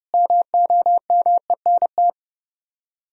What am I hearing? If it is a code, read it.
Morse code, heard as a keyed tone, about 37 wpm, 700 Hz. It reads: MOMENT
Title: Speed-Racing - Sets of 5 Words 30wpm